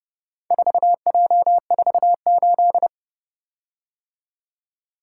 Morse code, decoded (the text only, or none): 4J48